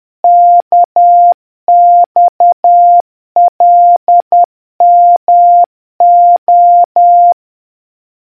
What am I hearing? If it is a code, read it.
Morse code, heard as a keyed tone, about 10 wpm, 700 Hz. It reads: KXLMO